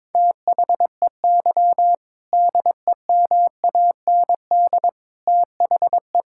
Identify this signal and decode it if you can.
Morse code, heard as a keyed tone, about 22 wpm, 700 Hz. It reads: THEY DEMAND THE
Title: Sentences from Top 1000 Words - Review Part 5 - 22wpm